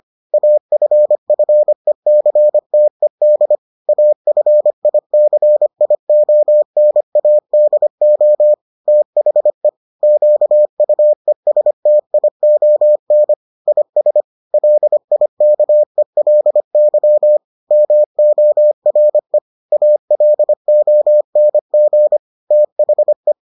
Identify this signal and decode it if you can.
Morse code, heard as a keyed tone, about 25 wpm, 600 Hz. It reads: AFFECTED AFICIONADO THE QUESTION IS LIKELY MORE ALONG THE